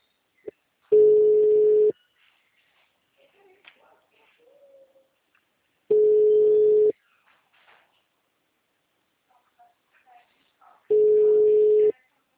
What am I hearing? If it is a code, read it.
Ringback tone (Europe, 425 Hz), with speech.
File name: out- -2001- - - .84211